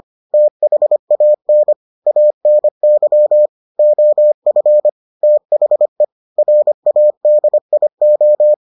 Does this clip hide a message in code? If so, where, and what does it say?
Yes: Morse code from 0.3 to 8.5 s, heard as a keyed tone, about 25 wpm, 600 Hz. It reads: THAN ANY OF THE RADIO